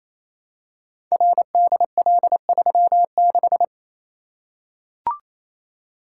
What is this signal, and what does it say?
Morse code, heard as a keyed tone, about 29 wpm, 700 Hz. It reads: RDL36
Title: Code Groups - 5 Letter:Numbers - 28wpm